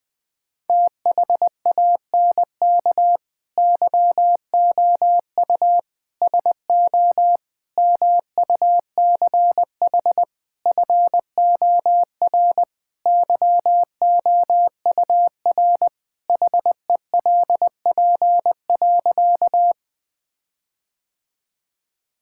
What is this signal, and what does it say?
Morse code, heard as a keyed tone, about 20 wpm, 700 Hz. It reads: THANK YOU SO MUCH FOR YOUR HELP.
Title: Sentences with 4-Word Phrases - 20wpm